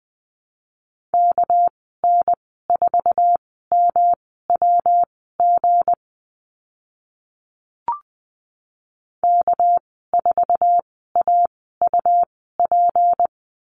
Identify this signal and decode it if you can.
Morse code, heard as a keyed tone, about 20 wpm, spaced as for 14 wpm, 700 Hz. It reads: KN4MWG K4AUP
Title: 219 Lesson 35g - Number 0 - Call Signs 20wpm